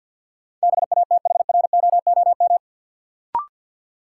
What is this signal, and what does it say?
Morse code, heard as a keyed tone, about 51 wpm, 700 Hz. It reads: BATHROOM
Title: Sets of 5 Words Encore - New Word Drill - 50wpm